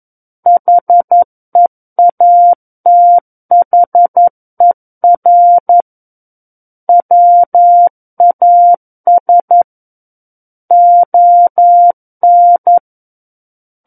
Morse code, heard as a keyed tone, about 11 wpm, 700 Hz. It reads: HEATHER WAS ON
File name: Book One / 02-Chapter 2 - 11 WPM